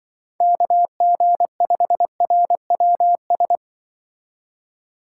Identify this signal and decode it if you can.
Morse code, heard as a keyed tone, about 25 wpm, 700 Hz. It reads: KG5RWS
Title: Speed Racing - USA 2x3 Call Signs 20wpm